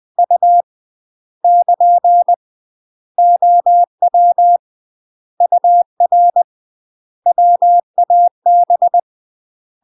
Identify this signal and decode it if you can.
Morse code, heard as a keyed tone, about 20 wpm, 700 Hz. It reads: U <KN> OW UR WAB
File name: UK MASTER QSOs TRAINING / 20WPM / UK MASTER QSOs 0012